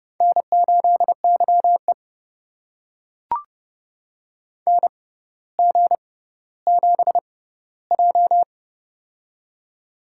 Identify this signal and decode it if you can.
Morse code, heard as a keyed tone, about 30 wpm, spaced as for 15 wpm, 700 Hz. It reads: N8YE NG7J
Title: USA Call Signs - 4 Characters - 6x Character Spacing - 30wpm